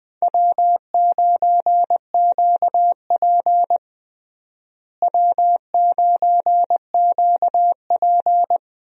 Morse code, heard as a keyed tone, about 20 wpm, 700 Hz. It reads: W9QP W9QP
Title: USA Call Signs - 4 Characters - Repeated 3x - 20wpm